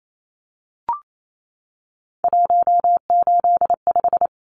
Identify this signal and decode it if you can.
Morse code, heard as a keyed tone, about 28 wpm, 700 Hz. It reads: 185
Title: Numbers - 3-Digit 28wpm